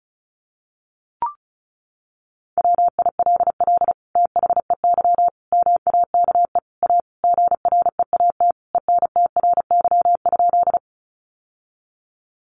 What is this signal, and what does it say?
Morse code, heard as a keyed tone, about 34 wpm, 700 Hz. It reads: WILL THEY MAKE A GREAT ENTRY?
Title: Sentences from Top 1500 Words 35wpm